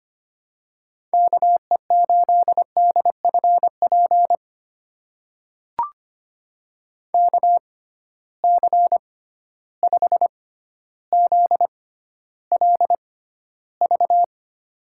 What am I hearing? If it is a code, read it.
Morse code, heard as a keyed tone, about 25 wpm, spaced as for 13 wpm, 700 Hz. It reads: KE8DFP KC5ZLV
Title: USA Call Signs - 6 Characters - 6x Character Spacing - 25wpm